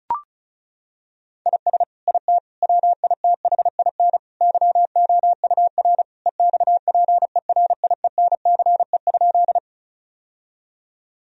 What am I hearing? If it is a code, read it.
Morse code, heard as a keyed tone, about 36 wpm, 700 Hz. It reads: IS IT WITHIN YOUR EXPERIENCE?
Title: Sentences from Top 1000 Words - Review Part 4 - 35wpm